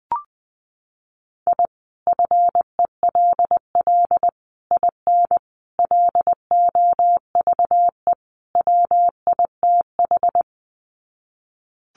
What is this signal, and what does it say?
Morse code, heard as a keyed tone, about 20 wpm, 700 Hz. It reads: I FELL IN LOVE WITH
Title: Sets of 5 Words Encore 20wpm